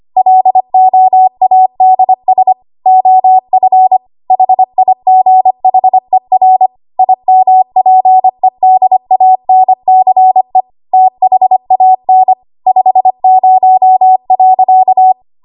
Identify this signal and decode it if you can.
Morse code, heard as a keyed tone, about 25 wpm, 750 Hz. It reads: LOADS OF HIGHER IMPEDANCE THAN 50.
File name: Archive / 25 WPM / 25WPM